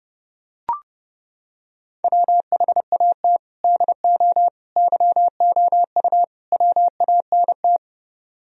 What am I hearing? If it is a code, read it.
Morse code, heard as a keyed tone, about 28 wpm, 700 Hz. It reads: WHAT DO YOU WANT